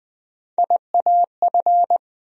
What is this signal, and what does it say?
Morse code, heard as a keyed tone, about 20 wpm, 700 Hz. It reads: IAF
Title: USA Call Sign Postfixes - 3 Letters 20wpm